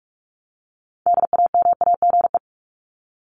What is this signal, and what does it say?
Morse code, heard as a keyed tone, about 44 wpm, 700 Hz. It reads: DAMAGE
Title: Top 1001-1250 Words 45wpm